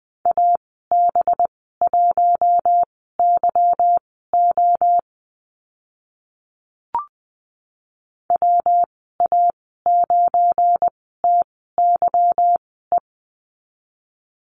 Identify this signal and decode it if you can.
Morse code, heard as a keyed tone, about 20 wpm, spaced as for 15 wpm, 700 Hz. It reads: AB1YO WA9TYE
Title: 186 Lesson 30g - Letter b - Call Signs 20wpm